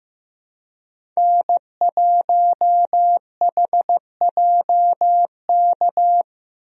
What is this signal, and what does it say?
Morse code, heard as a keyed tone, about 15 wpm, 700 Hz. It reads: N1HJK
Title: USA Call Signs - 5 Characters - Repeated 2x - 15wpm